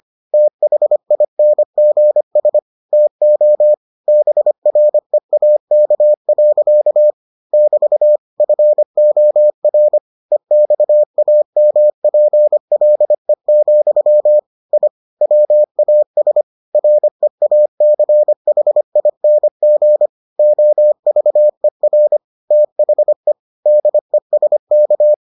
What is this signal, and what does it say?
Morse code, heard as a keyed tone, about 25 wpm, 600 Hz. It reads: THINGS TO BREAK. <BT> FOR EXAMPLE, I WAS REACHING OVER THE DESK